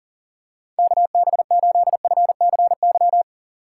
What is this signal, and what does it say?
Morse code, heard as a keyed tone, about 40 wpm, 700 Hz. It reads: KB8FCY
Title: USA Call Signs - 6 Characters - 6x Character Spacing - 40wpm